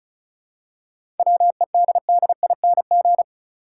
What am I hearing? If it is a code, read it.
Morse code, heard as a keyed tone, about 36 wpm, 700 Hz. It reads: WEDDING
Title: Seven Letter Words - 35wpm